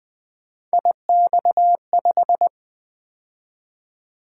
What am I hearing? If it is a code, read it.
Morse code, heard as a keyed tone, about 20 wpm, 700 Hz. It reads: IX5
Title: Three Letter-Number Combinations - 20wpm